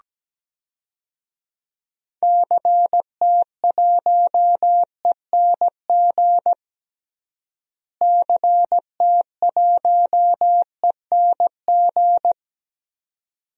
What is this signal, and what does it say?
Morse code, heard as a keyed tone, about 17 wpm, 700 Hz. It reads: CT1ENG CT1ENG
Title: DXCC-100 Call Signs - 6 Characters - Repeated 2x - 17wpm